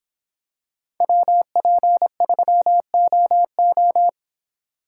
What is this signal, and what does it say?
Morse code, heard as a keyed tone, about 24 wpm, 700 Hz. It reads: WP3OO